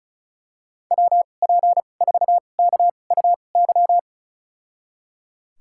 Morse code, heard as a keyed tone, about 36 wpm, spaced as for 25 wpm, 700 Hz. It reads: WP4KUY
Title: USA Call Signs - 6 Characters - 2x Character Spacing - 35wpm